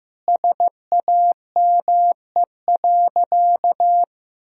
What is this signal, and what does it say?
Morse code, heard as a keyed tone, about 15 wpm, 700 Hz. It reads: SAME.
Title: Sentences from Top 1000 Words - Review Part 1 - 15wpm